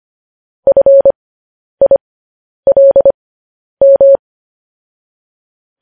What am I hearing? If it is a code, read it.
Morse code, heard as a keyed tone, about 25 wpm, spaced as for 10 wpm, 550 Hz. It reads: FILM
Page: news-headlines-in-morse-code-at-10-wpm